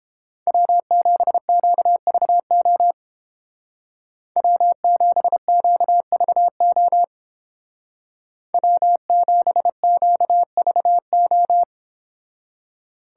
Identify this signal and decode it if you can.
Morse code, heard as a keyed tone, about 30 wpm, 700 Hz. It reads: W7QVO W7QVO W7QVO